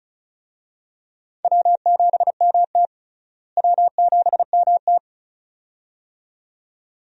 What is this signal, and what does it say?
Morse code, heard as a keyed tone, about 36 wpm, 700 Hz. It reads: W7MT W7MT
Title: USA Call Signs - 4 Characters - Repeated 2x - 35wpm